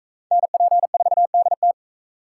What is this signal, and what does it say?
Morse code, heard as a keyed tone, about 41 wpm, 700 Hz. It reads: NP4DT